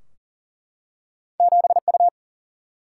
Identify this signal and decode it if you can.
Morse code, heard as a keyed tone, about 40 wpm, 700 Hz. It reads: 7U
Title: Single-Letter Single-Number 40wpm